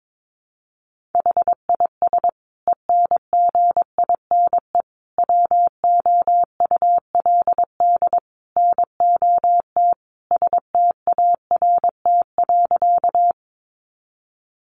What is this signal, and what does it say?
Morse code, heard as a keyed tone, about 22 wpm, 700 Hz. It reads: HIS ENGINE WOULD NOT START.
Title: Sentences from Top 1500 Words 22wpm